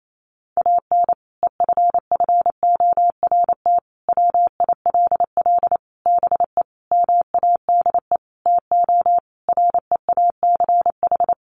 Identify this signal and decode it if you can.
Morse code, heard as a keyed tone, about 28 wpm, 700 Hz. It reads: AN EFFORT WILL BE MADE TO REACH
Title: Sentences from Top 1500 Words 28wpm